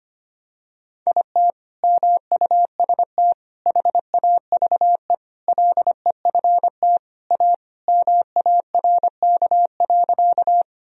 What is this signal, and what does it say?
Morse code, heard as a keyed tone, about 25 wpm, 700 Hz. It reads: IT MUST HAVE LEFT A MARK.
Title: Sentences from Top 1000 Words - Review Part 1 - 25wpm